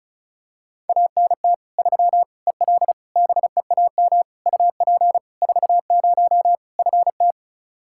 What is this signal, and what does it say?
Morse code, heard as a keyed tone, about 36 wpm, 700 Hz. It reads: ANT 3 EL BEAM UP 40 FT
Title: Practice QSO - Exchange 2 - Info - 35wpm